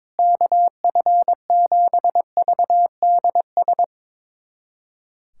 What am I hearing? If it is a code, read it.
Morse code, heard as a keyed tone, about 21 wpm, 700 Hz. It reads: KF7VDS
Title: USA Callsign DE Callsign - 6 Characters - 22wpm